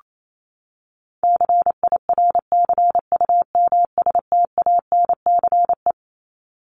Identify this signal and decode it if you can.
Morse code, heard as a keyed tone, about 28 wpm, 700 Hz. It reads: CIRCUMSTANCE
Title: ES:S Suffix - Top 100 Words with Base Words 28wpm